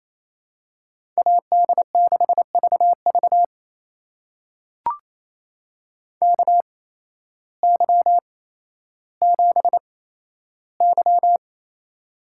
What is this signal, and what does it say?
Morse code, heard as a keyed tone, about 28 wpm, spaced as for 13 wpm, 700 Hz. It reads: AD6VV KY7Y
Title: USA Call Signs - 8x Character Spacing - 28wpm